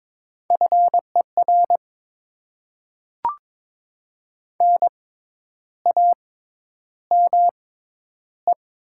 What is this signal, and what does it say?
Morse code, heard as a keyed tone, about 22 wpm, spaced as for 10 wpm, 690 Hz. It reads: FER NAME
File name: Top 20 QSO Elements - 6x Character Spacing - 22wpm